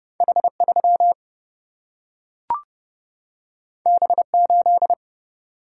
Morse code, heard as a keyed tone, about 31 wpm, 700 Hz. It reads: H3 B8
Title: Single-Letter Single-Number 30wpm